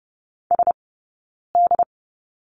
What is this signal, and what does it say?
Morse code, heard as a keyed tone, about 30 wpm, spaced as for 9 wpm, 690 Hz. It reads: SD
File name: Lesson 17d - Letter u - Two Characters 30wpm